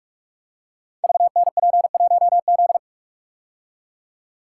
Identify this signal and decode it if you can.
Morse code, heard as a keyed tone, about 44 wpm, 700 Hz. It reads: VNP1Z
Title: Code Groups - 5 Letter:Numbers - 45wpm